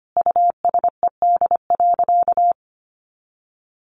Morse code, heard as a keyed tone, about 25 wpm, 700 Hz. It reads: USED.
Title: Sentences from Top 1500 Words 25wpm